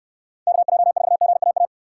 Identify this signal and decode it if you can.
Morse code, heard as a keyed tone, about 69 wpm, 700 Hz. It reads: KJ4GAA